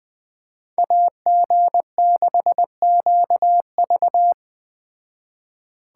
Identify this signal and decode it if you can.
Morse code, heard as a keyed tone, about 21 wpm, 700 Hz. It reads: AG6QV
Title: Speed Racing - USA 2x2 Call Signs 17wpm